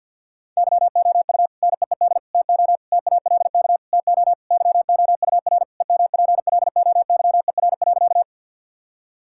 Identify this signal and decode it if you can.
Morse code, heard as a keyed tone, about 50 wpm, 700 Hz. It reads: YOU NEED TO TALK TO YOUR EMPLOYER.
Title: Sentences from Top 1600 Words 50wpm